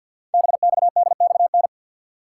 Morse code, heard as a keyed tone, about 51 wpm, 700 Hz. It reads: BXDXN
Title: Code Groups - 5 Letter:Numbers - 50wpm